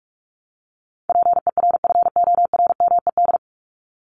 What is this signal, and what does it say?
Morse code, heard as a keyed tone, about 44 wpm, 700 Hz. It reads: PERFORMED
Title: ED Suffix - Top 100 Words with Base Words 45wpm